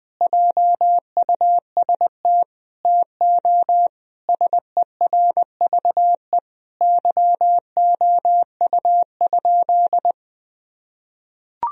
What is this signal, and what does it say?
Morse code, heard as a keyed tone, about 20 wpm, 700 Hz. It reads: JUST TO SERVE YOU?
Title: Sentences from Top 1000 Words - Review Part 2 - 20wpm